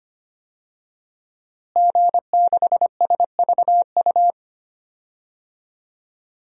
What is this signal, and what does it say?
Morse code, heard as a keyed tone, about 25 wpm, 700 Hz. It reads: G6SVU
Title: DXCC-100 Call Signs - 5 Characters - Repeated 3x - 25wpm